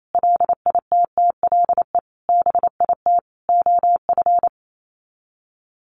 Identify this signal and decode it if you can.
Morse code, heard as a keyed tone, about 28 wpm, 700 Hz. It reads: LITTLE BIT OF
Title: Sets of 4 Words 28wpm